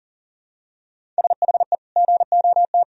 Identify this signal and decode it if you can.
Morse code, heard as a keyed tone, about 40 wpm, 700 Hz. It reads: SHE GOT
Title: Sentences from Top 500 Words and Abbreviations 40wpm